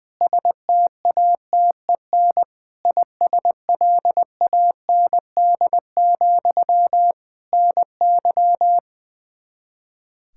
Morse code, heard as a keyed tone, about 20 wpm, 700 Hz. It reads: STATEN ISLAND, NY
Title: Practice QSO - Exchange 1 - Beginning - 20wpm